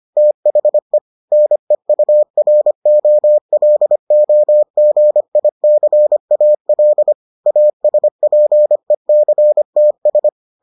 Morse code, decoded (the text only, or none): THE NEUROLOGICAL ASPECTS